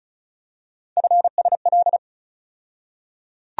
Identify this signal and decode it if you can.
Morse code, heard as a keyed tone, about 34 wpm, 700 Hz. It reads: FSL
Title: USA Call Sign Postfixes - 3 Letters 35wpm